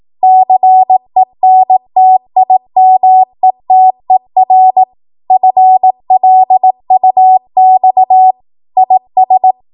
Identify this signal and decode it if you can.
Morse code, heard as a keyed tone, about 18 wpm, 750 Hz. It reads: CENTIMETER FLUX IS
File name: Morse / 18WPM